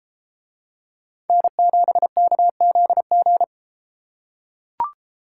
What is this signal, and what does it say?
Morse code, heard as a keyed tone, about 33 wpm, 700 Hz. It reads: N7KZG